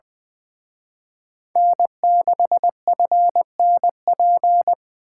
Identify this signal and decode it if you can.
Morse code, heard as a keyed tone, about 24 wpm, 700 Hz. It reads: N6FNP
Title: Speed Racing - USA 1x3 Call Signs 20wpm